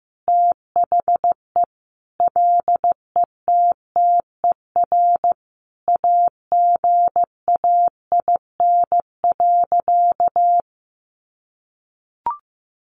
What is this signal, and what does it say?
Morse code, heard as a keyed tone, about 15 wpm, 700 Hz. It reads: THE LETTER AGAIN.
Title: Sentences from Top 300 Words 15wpm